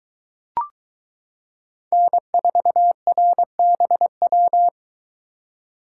Morse code, heard as a keyed tone, about 23 wpm, 700 Hz. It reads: N4RBW